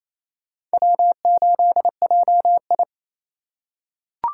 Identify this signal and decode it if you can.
Morse code, heard as a keyed tone, about 28 wpm, 700 Hz. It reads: W8JI
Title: USA Call Signs - 4 Characters - 6x Character Spacing - 28wpm